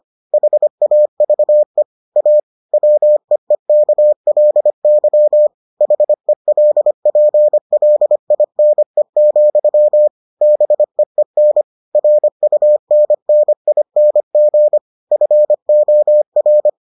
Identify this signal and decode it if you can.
Morse code, heard as a keyed tone, about 25 wpm, 600 Hz. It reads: HAVE A WEEKLY HELPLINE, BEEN RUNNING FOR